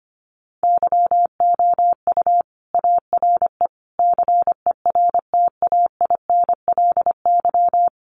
Morse code, heard as a keyed tone, about 25 wpm, 700 Hz. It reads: YOU ARE CERTAINLY
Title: Sentences from Top 1200 Words 25wpm